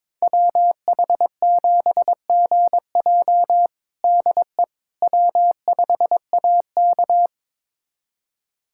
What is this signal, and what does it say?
Morse code, heard as a keyed tone, about 22 wpm, 700 Hz. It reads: WH7GJ DE W5AK
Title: USA Callsign DE Callsign 22wpm